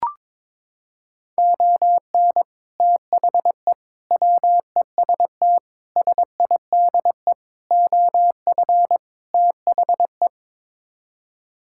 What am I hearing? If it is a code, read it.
Morse code, heard as a keyed tone, about 22 wpm, 700 Hz. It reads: ON THE WEST SIDE OF THE
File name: practice / 22wpm / Sets of 6 Words - 22wpm